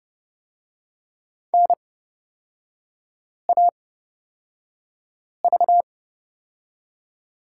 Morse code, heard as a keyed tone, about 30 wpm, spaced as for 5 wpm, 700 Hz. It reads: NAV
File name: Lesson 38e - Letter q - Three Characters 30wpm